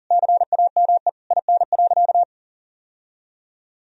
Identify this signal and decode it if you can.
Morse code, heard as a keyed tone, about 40 wpm, 700 Hz. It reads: CAME IN.
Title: Sentences from Top 1000 Words - Review Part 3 - 40wpm